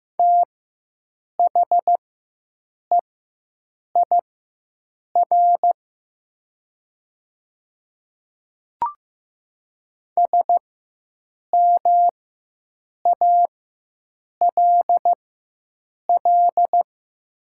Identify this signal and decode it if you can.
Morse code, heard as a keyed tone, about 15 wpm, spaced as for 7 wpm, 700 Hz. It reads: THEIR SMALL
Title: 4x Character Spacing - Top 300 Words - 15wpm